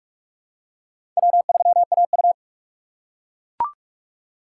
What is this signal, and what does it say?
Morse code, heard as a keyed tone, about 44 wpm, 690 Hz. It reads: W3AU
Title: USA Call Signs - 4 Characters - 3x Character Spacing - 45wpm